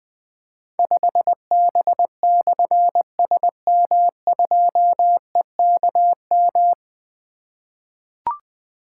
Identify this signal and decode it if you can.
Morse code, heard as a keyed tone, about 20 wpm, 700 Hz. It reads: 5B/SM2EKM
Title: Difficult Call Signs 20wpm